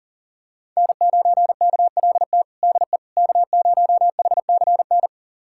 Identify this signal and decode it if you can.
Morse code, heard as a keyed tone, about 40 wpm, 700 Hz. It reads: N9KLT DE K0HCN
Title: USA Callsign DE Callsign - 5 Characters - 40wpm